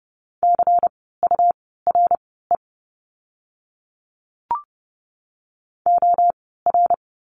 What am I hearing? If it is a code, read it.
Morse code, heard as a keyed tone, about 30 wpm, spaced as for 17 wpm, 700 Hz. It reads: CURE OR
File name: Lesson 17f - Letter u - Words 30wpm